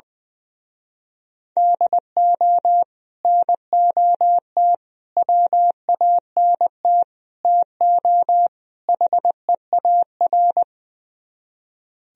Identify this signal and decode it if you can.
Morse code, heard as a keyed tone, about 20 wpm, 700 Hz. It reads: DO NOT WANT TO HEAR
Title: Speed-Racing - Sets of 5 Words 20wpm